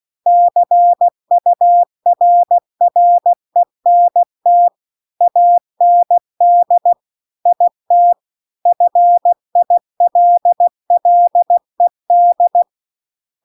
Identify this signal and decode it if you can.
Morse code, heard as a keyed tone, about 16 wpm, 700 Hz. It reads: CURRENT AND IT FILLED
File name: CW-The-Big-Twoheaded-River-2-16WPM